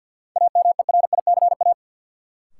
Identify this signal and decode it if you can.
Morse code, heard as a keyed tone, about 53 wpm, 700 Hz. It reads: AMERICA
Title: Sets of 5 Words - New Word Drill - 50wpm